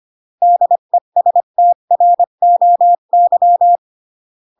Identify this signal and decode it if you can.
Morse code, heard as a keyed tone, about 25 wpm, 700 Hz. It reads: DESTROY